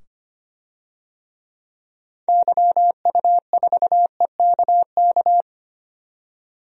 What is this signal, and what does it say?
Morse code, heard as a keyed tone, about 25 wpm, 700 Hz. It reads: YU4EKK
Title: DXCC-100 Call Signs - 6 Characters - 25wpm